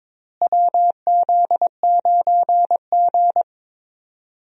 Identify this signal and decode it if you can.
Morse code, heard as a keyed tone, about 22 wpm, 700 Hz. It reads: WZ9G
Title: USA Call Signs - 4 Characters - Repeated 3x - 22wpm